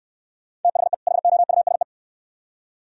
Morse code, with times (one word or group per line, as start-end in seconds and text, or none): 0.6-0.9 THE
1.1-1.8 HOUSE